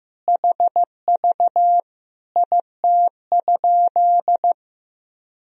Morse code, heard as a keyed tone, about 16 wpm, 700 Hz. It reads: HV IT?
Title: Sentences from Top 100 Words and Abbreviations 15wpm